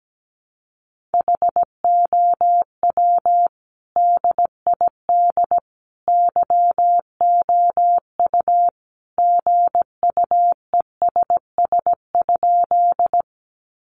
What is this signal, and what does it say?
Morse code, heard as a keyed tone, about 17 wpm, 700 Hz. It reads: HOW DID YOU GUESS?
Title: Sentences from Top 1500 Words 17wpm